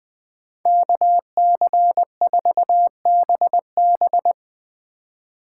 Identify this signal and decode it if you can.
Morse code, heard as a keyed tone, about 21 wpm, 700 Hz. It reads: KC4BB
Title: 2023 POTA Activator Call Signs - 20wpm